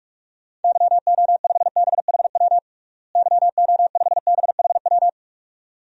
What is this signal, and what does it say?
Morse code, heard as a keyed tone, about 45 wpm, 700 Hz. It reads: YO5BHW YO5BHW
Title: DXCC-100 Call Signs - 6 Characters - Repeated 2x - 45wpm